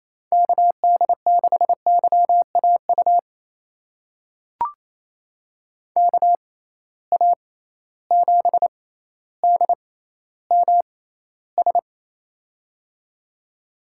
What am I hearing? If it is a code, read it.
Morse code, heard as a keyed tone, about 28 wpm, spaced as for 14 wpm, 700 Hz. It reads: KD6YAU KA7DMS
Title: USA Call Signs - 6 Characters - 6x Character Spacing - 28wpm